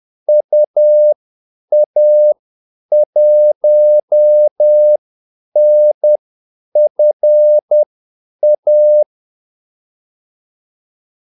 Morse code, heard as a keyed tone, about 10 wpm, spaced as for 8 wpm, 600 Hz. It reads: UA1NFA
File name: Callsigns-10-8wpm